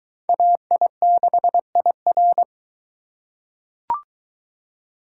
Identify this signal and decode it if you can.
Morse code, heard as a keyed tone, about 22 wpm, 700 Hz. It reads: AI6IR